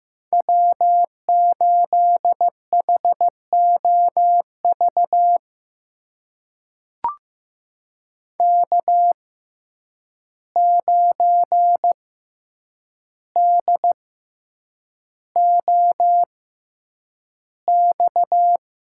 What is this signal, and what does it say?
Morse code, heard as a keyed tone, about 15 wpm, spaced as for 8 wpm, 700 Hz. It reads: W8HOV K9DOX
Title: USA Call Signs - 5 Characters - 6x Character Spacing - 15wpm